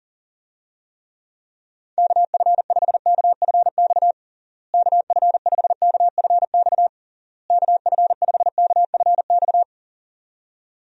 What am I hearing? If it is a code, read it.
Morse code, heard as a keyed tone, about 40 wpm, 700 Hz. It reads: KF5KFX KF5KFX KF5KFX
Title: USA Call Signs - 6 Characters - Repeated 3x - 40wpm